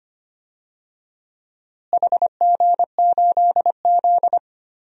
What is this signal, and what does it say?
Morse code, heard as a keyed tone, about 25 wpm, 700 Hz. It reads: HG8Z
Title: DXCC-100 Call Signs - 4 Characters - Repeated 3x - 25wpm